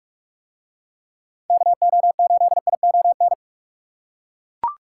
Morse code, heard as a keyed tone, about 41 wpm, 700 Hz. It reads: KO8ION